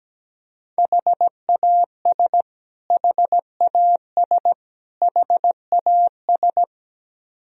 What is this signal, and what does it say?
Morse code, heard as a keyed tone, about 17 wpm, 700 Hz. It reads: HAS HAS HAS
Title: Top 100 Words Repeated 17wpm